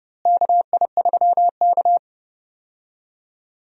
Morse code, heard as a keyed tone, about 31 wpm, 700 Hz. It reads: KI3K
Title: USA Call Signs - 4 Characters - Repeated 3x - 30wpm